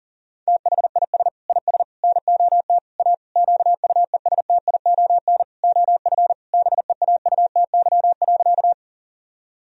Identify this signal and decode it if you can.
Morse code, heard as a keyed tone, about 37 wpm, 700 Hz. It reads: THIS IS NOT A QUESTION OF BEAUTY.
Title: Sentences from Top 1000 Words - Review Part 2 - 40wpm